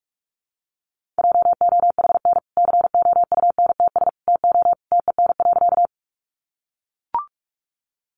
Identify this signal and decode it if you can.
Morse code, heard as a keyed tone, about 44 wpm, 700 Hz. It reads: JOHN COUNTS TO TEN.
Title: Easy Phrases 45wpm